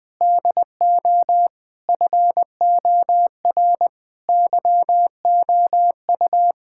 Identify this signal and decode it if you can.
Morse code, heard as a keyed tone, about 20 wpm, 700 Hz. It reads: DO FOR YOU
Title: Sets of 9 Words - 20wpm